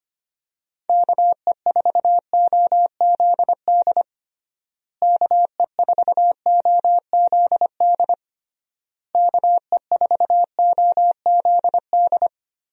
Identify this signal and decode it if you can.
Morse code, heard as a keyed tone, about 25 wpm, 700 Hz. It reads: KE4OZD KE4OZD KE4OZD
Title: USA Call Signs - 6 Characters - Repeated 3x - 25wpm